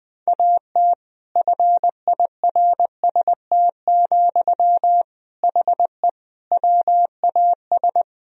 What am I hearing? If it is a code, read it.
Morse code, heard as a keyed tone, about 20 wpm, 700 Hz. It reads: AT FIRST, HE WAS
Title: Sentences from Top 1000 Words - Review Part 4 - 20wpm